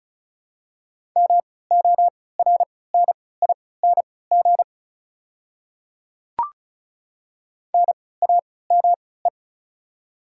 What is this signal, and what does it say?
Morse code, heard as a keyed tone, about 36 wpm, spaced as for 20 wpm, 700 Hz. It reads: MORNING NAME
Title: 3x Character Spacing - Top 500 Words - 35wpm